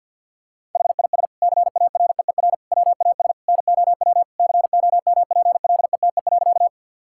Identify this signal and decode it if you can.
Morse code, heard as a keyed tone, about 50 wpm, 700 Hz. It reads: HIS CAREER WAS NOW COMPLETE.
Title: Sentences from Top 1000 Words - Review Part 6 - 50wpm